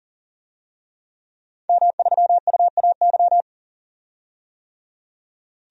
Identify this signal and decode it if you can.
Morse code, heard as a keyed tone, about 40 wpm, 700 Hz. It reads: M3UAY
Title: DXCC-100 Call Signs - 5 Characters - 40wpm